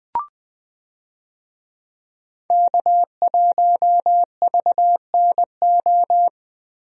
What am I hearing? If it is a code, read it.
Morse code, heard as a keyed tone, about 20 wpm, 700 Hz. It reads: K1VNO
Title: USA Call Signs - 5 Characters - Repeated 2x - 20wpm